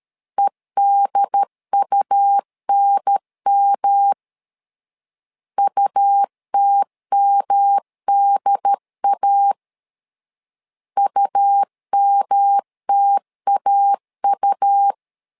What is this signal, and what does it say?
Morse code, heard as a keyed tone, about 13 wpm, 800 Hz. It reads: EDUNM UTMDA UMTAU
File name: LEZIONI / 3 MWRO / 3 M 12wpm 1541